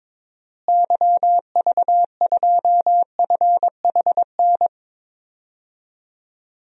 Morse code, heard as a keyed tone, about 22 wpm, 700 Hz. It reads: YV2FHN
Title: DXCC-100 Call Signs - 6 Characters - Repeated 3x - 22wpm